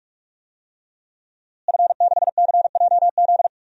Morse code, heard as a keyed tone, about 41 wpm, 700 Hz. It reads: F6CJZ
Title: DXCC-100 Call Signs - 5 Characters - Repeated 3x - 45wpm